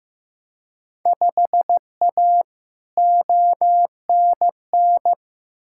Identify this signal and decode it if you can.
Morse code, heard as a keyed tone, about 15 wpm, 700 Hz. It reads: 5A ONN
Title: Field Day Exchanges Encore - 15wpm